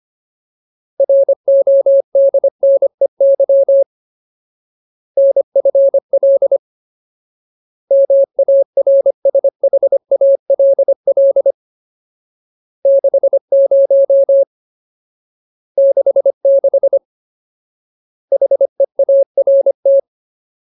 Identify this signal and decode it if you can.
Morse code, heard as a keyed tone, about 25 wpm, 550 Hz. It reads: RODNEY NFL MARSHALL 60 66 HEART
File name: Ultimate Word List / 25WPM Ultimate